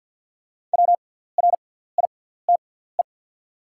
Morse code, heard as a keyed tone, about 50 wpm, spaced as for 17 wpm, 700 Hz. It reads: WRITE
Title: 6x Character Spacing - Top 100 Words - 50wpm